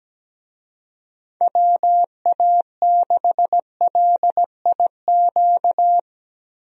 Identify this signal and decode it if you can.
Morse code, heard as a keyed tone, about 17 wpm, 700 Hz. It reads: WA6LIQ
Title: USA Call Signs - 6 Characters - Repeated 2x - 17wpm